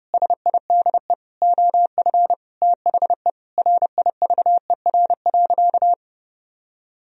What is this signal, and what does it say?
Morse code, heard as a keyed tone, about 30 wpm, 700 Hz. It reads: SIDE OF THE RIVER.
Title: Sentences from Top 1000 Words - Review Part 1 - 30wpm